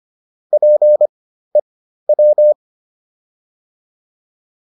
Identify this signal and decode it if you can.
Morse code, heard as a keyed tone, about 25 wpm, spaced as for 13 wpm, 600 Hz. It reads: PEW